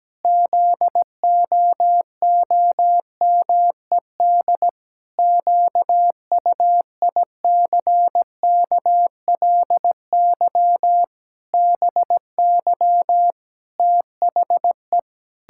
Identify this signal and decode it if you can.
Morse code, heard as a keyed tone, about 17 wpm, 700 Hz. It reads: ZOOMED QUICKLY BY THE
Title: Pangram Challenge v2 - 17wpm